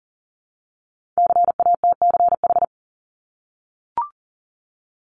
Morse code, heard as a keyed tone, about 41 wpm, 700 Hz. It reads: CATCH
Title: Multi-letter Phonemes Repeated 3x 40wpm